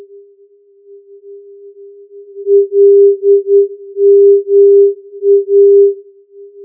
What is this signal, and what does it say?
Morse code, heard as a keyed tone, about 9 wpm, 400 Hz. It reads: LMA
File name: filtered morse